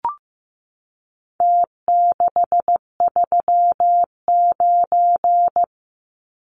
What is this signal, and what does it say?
Morse code, heard as a keyed tone, about 15 wpm, 700 Hz. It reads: T639
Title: Numbers - Contest Style - 4-Digit 15wpm